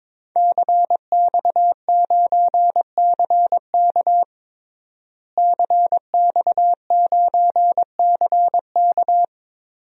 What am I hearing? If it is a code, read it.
Morse code, heard as a keyed tone, about 22 wpm, 700 Hz. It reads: CX9CK CX9CK
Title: DXCC-100 Call Signs - 5 Characters - Repeated 2x - 22wpm